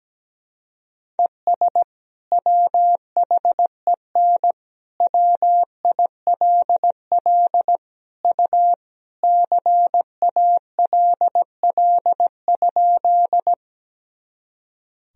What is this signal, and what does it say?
Morse code, heard as a keyed tone, about 17 wpm, 700 Hz. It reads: ES WHEN WILL U CALL?
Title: Sentences from Top 100 Words and Abbreviations 17wpm